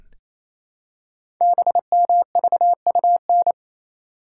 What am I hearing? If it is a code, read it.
Morse code, heard as a keyed tone, about 28 wpm, 700 Hz. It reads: BMVUN
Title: Code Groups - 5 Letters - 28wpm